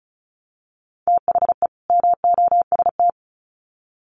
Morse code, heard as a keyed tone, about 34 wpm, 700 Hz. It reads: THE MOST
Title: Sets of 2 Words 35wpm